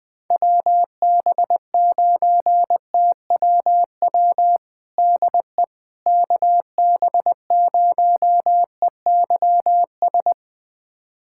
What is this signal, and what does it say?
Morse code, heard as a keyed tone, about 20 wpm, 700 Hz. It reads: WB9TWW DE KB0EYS
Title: USA Callsign DE Callsign - 6 Characters - 20wpm